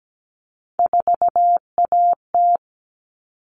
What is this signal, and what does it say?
Morse code, heard as a keyed tone, about 17 wpm, 700 Hz. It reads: 4AT
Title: All Cut Numbers - 3-Digit 17wpm